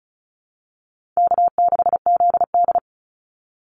Morse code, heard as a keyed tone, about 34 wpm, 700 Hz. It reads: K6ZD
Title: Easy Call Signs Repeated 2x - 35wpm